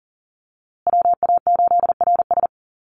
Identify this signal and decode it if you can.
Morse code, heard as a keyed tone, about 40 wpm, 700 Hz. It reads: WA8RS
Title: USA Call Signs Repeated 3x 40wpm